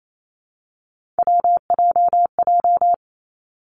Morse code, heard as a keyed tone, about 28 wpm, 700 Hz. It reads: WJJ
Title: Three Challenging Letter-Number Combinations 28wpm